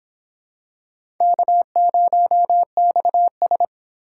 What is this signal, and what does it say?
Morse code, heard as a keyed tone, about 26 wpm, 700 Hz. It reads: K0XS